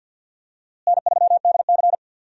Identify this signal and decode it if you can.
Morse code, heard as a keyed tone, about 49 wpm, 700 Hz. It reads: N3DC
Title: USA Call Signs - 4 Characters - Repeated 2x - 50wpm